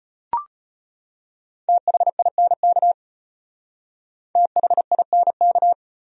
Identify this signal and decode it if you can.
Morse code, heard as a keyed tone, about 32 wpm, 700 Hz. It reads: THINK THINK